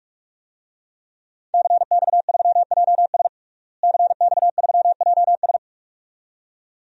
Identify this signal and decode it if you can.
Morse code, heard as a keyed tone, about 46 wpm, 700 Hz. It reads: CX3JS CX3JS
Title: DXCC-100 Call Signs - 5 Characters - Repeated 2x - 45wpm